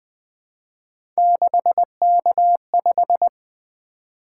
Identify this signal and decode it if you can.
Morse code, heard as a keyed tone, about 20 wpm, 700 Hz. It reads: 6K5
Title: Three Letter-Number Combinations - 20wpm